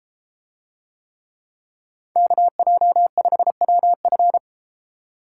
Morse code, heard as a keyed tone, about 33 wpm, 700 Hz. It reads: KJ5WF